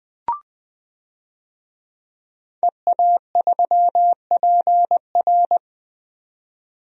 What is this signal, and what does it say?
Morse code, heard as a keyed tone, about 20 wpm, 700 Hz. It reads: EA3PR
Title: DXCC-100 Call Signs - 5 Characters - 20wpm